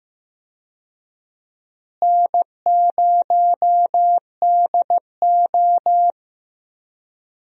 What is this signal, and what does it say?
Morse code, heard as a keyed tone, about 15 wpm, 700 Hz. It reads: N0DO
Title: USA Call Signs - 4 Characters - Repeated 2x - 15wpm